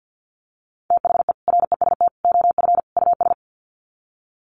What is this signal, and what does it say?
Morse code, heard as a keyed tone, about 50 wpm, 700 Hz. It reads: THE REST OF US
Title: Sets of 4 Words 50wpm